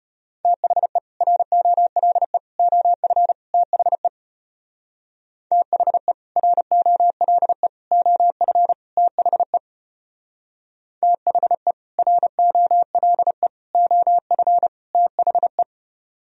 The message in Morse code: THE ROLE OF THE THE ROLE OF THE THE ROLE OF THE

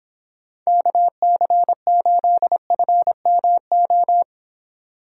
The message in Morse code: KC8FMO

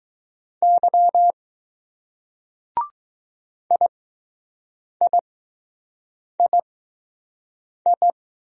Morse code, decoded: Y IIII

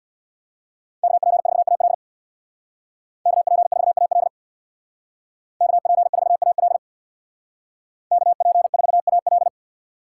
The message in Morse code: KP4AL KP4AL KP4AL KP4AL

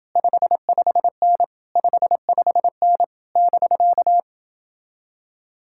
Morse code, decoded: 55N 55N <BK>

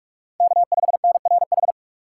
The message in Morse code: K5NRH

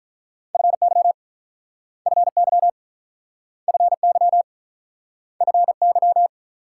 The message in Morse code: FY FY FY FY

